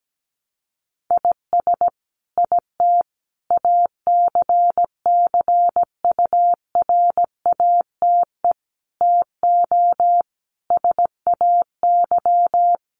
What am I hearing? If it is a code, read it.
Morse code, heard as a keyed tone, about 17 wpm, 700 Hz. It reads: IS IT ACCURATE TO SAY
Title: Sentences from Top 1200 Words 17wpm